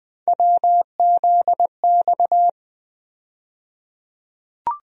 Morse code, heard as a keyed tone, about 20 wpm, 700 Hz. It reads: WZX
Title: USA Call Sign Postfixes - 3 Letters 20wpm